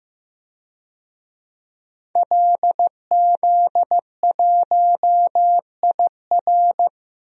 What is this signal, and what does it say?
Morse code, heard as a keyed tone, about 15 wpm, 700 Hz. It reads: LZ1IR
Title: DXCC-100 Call Signs - 5 Characters - Repeated 3x - 15wpm